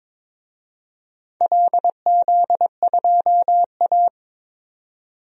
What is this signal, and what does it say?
Morse code, heard as a keyed tone, about 22 wpm, 700 Hz. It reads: LZ2A
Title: DXCC-100 Call Signs - 4 Characters - 22wpm